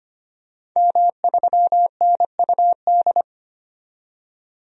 Morse code, heard as a keyed tone, about 25 wpm, 700 Hz. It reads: M3NUD
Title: DXCC-100 Call Signs - 5 Characters - 25wpm